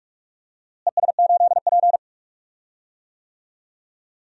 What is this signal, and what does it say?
Morse code, heard as a keyed tone, about 46 wpm, 700 Hz. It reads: ES8P
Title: DXCC-100 Call Signs - 4 Characters - Repeated 3x - 45wpm